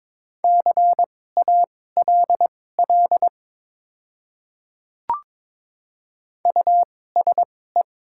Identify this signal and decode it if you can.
Morse code, heard as a keyed tone, about 22 wpm, spaced as for 16 wpm, 700 Hz. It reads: CALL USE
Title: 2x Character Spacing - Top 100 Words - 22wpm